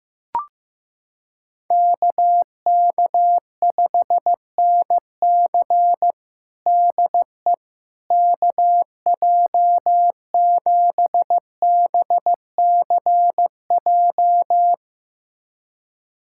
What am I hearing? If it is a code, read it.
Morse code, heard as a keyed tone, about 15 wpm, 700 Hz. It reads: KK5NC DE KJ7BCJ
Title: USA Callsign DE Callsign 15wpm